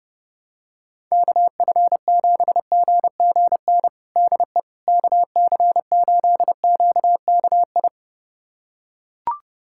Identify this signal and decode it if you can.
Morse code, heard as a keyed tone, about 30 wpm, 700 Hz. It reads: KF7GGN DE KC8QKI
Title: USA Callsign DE Callsign 30wpm